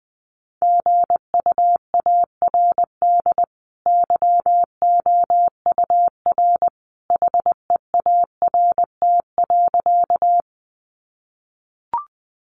Morse code, heard as a keyed tone, about 20 wpm, 700 Hz. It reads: GUARD YOUR HEART.
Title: Sentences from Top 1200 Words 20wpm